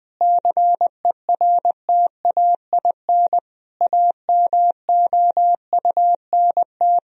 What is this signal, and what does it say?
Morse code, heard as a keyed tone, about 19 wpm, 700 Hz. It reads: CERTAIN AMOUNT
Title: Sentences from Top 1000 Words - Review Part 4 - 20wpm